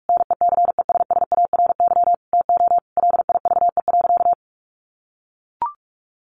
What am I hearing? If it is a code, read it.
Morse code, heard as a keyed tone, about 44 wpm, 700 Hz. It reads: NECESSARY TO LIVE.
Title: Sentences from Top 800 Words 45wpm